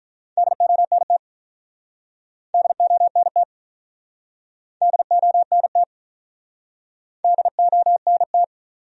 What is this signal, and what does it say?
Morse code, heard as a keyed tone, about 47 wpm, 700 Hz. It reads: DONT DONT DONT DONT